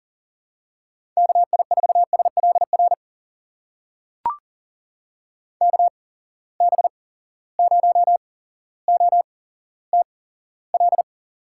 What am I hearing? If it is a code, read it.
Morse code, heard as a keyed tone, about 42 wpm, spaced as for 17 wpm, 700 Hz. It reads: KI4SLR KB0OTL
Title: USA Call Signs - 6 Characters - 8x Character Spacing - 40wpm